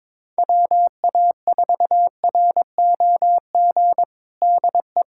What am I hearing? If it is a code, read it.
Morse code, heard as a keyed tone, about 22 wpm, 700 Hz. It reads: WA4ROG DE
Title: USA Callsign DE Callsign - 6 Characters - 22wpm